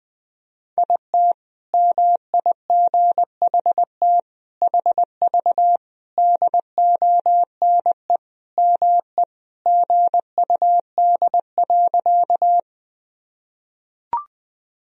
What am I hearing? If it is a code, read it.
Morse code, heard as a keyed tone, about 20 wpm, 700 Hz. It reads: IT MIGHT HV DONE ME GUD.
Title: Sentences from Top 500 Words and Abbreviations 20wpm